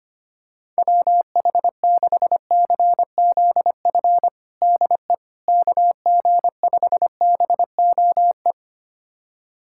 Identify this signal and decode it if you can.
Morse code, heard as a keyed tone, about 25 wpm, 700 Hz. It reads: WH6CZF DE KG5BOE
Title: USA Callsign DE Callsign - 6 Characters - 25wpm